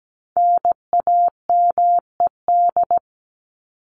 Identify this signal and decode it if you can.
Morse code, heard as a keyed tone, about 17 wpm, 700 Hz. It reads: NAMED
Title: ED Suffix - Top 100 Words with Base Words 17wpm